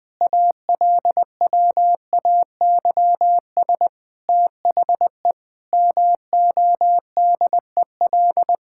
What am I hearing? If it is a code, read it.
Morse code, heard as a keyed tone, about 20 wpm, 700 Hz. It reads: ALWAYS THE MODEL
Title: Sentences from Top 1000 Words - Review Part 5 - 20wpm